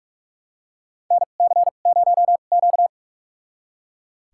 Morse code, heard as a keyed tone, about 41 wpm, spaced as for 31 wpm, 700 Hz. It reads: NC0Q
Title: USA Call Signs - 4 Characters - 2x Character Spacing - 45wpm